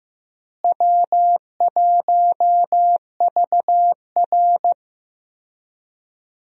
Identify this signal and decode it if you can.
Morse code, heard as a keyed tone, about 15 wpm, 700 Hz. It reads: W1VR
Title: USA Call Signs - 4 Characters - Repeated 2x - 15wpm